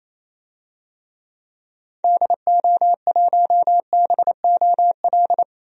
Morse code, heard as a keyed tone, about 28 wpm, 700 Hz. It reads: DO1BOL
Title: DXCC-100 Call Signs - 6 Characters - Repeated 2x - 28wpm